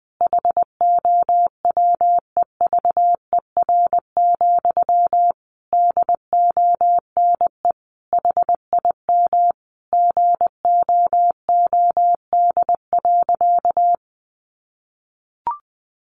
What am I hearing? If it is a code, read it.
Morse code, heard as a keyed tone, about 20 wpm, 700 Hz. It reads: HOWEVER, DONE HIM GOOD.
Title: Sentences from Top 1200 Words 20wpm